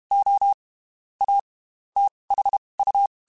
Morse code, heard as a keyed tone, about 32 wpm, spaced as for 23 wpm, 800 Hz. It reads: O A THU